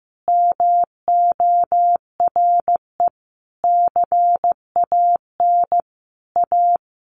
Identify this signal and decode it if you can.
Morse code, heard as a keyed tone, about 14 wpm, 700 Hz. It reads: MORE CAN A
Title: Sentences from Top 1100 Words 15wpm